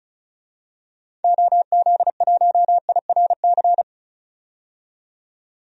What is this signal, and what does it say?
Morse code, heard as a keyed tone, about 34 wpm, 700 Hz. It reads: OZ1IRC
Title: DXCC-100 Call Signs - 6 Characters - 35wpm